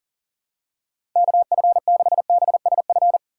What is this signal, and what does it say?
Morse code, heard as a keyed tone, about 40 wpm, 700 Hz. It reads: KF6BSF
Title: USA Call Signs - 6 Characters - 2x Character Spacing - 40wpm